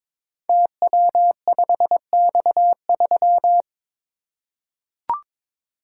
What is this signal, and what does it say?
Morse code, heard as a keyed tone, about 23 wpm, 690 Hz. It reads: TW5X3
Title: Code Groups - 5 Letter:Numbers - 22wpm